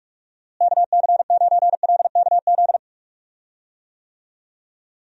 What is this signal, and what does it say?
Morse code, heard as a keyed tone, about 46 wpm, 700 Hz. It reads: KC9LKZ
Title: USA Call Signs - 6 Characters - Repeated 2x - 45wpm